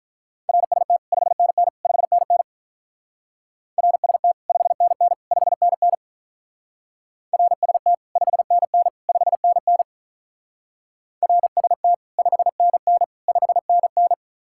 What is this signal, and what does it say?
Morse code, heard as a keyed tone, about 44 wpm, 700 Hz. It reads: RST 5NN 5NN RST 5NN 5NN RST 5NN 5NN RST 5NN 5NN